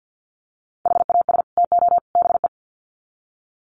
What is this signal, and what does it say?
Morse code, heard as a keyed tone, about 50 wpm, 700 Hz. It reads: HAS TO BE
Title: Sets of 3 Words 50wpm